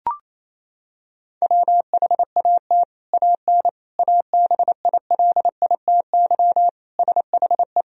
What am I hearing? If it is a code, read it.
Morse code, heard as a keyed tone, about 28 wpm, 700 Hz. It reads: WHAT AN ABILITY SHE
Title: Sentences from Top 1000 Words - Review Part 3 - 28wpm